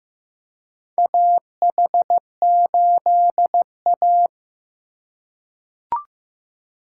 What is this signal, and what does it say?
Morse code, heard as a keyed tone, about 15 wpm, 700 Hz. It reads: AH8A
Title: USA Call Signs - 2x Character Spacing - 15wpm